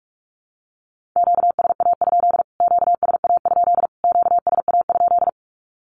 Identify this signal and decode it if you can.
Morse code, heard as a keyed tone, about 44 wpm, 700 Hz. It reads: QSA? QSA? QSA?
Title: Q-Codes Repeated 3x 45wpm